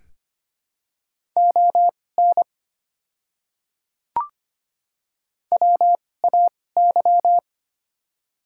Morse code, heard as a keyed tone, about 25 wpm, spaced as for 18 wpm, 700 Hz. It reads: ON WAY
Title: 2x Character Spacing - Top 100 Words - 25wpm